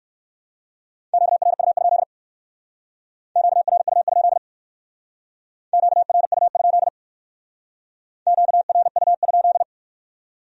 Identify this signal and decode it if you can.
Morse code, heard as a keyed tone, about 54 wpm, 700 Hz. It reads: QRU? QRU? QRU? QRU?
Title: Speed-Racing Q-Codes - 45wpm